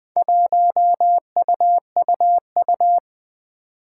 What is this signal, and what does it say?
Morse code, heard as a keyed tone, about 20 wpm, 700 Hz. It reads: 1UUU
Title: Four Challenging Letter-Number Combinations 20wpm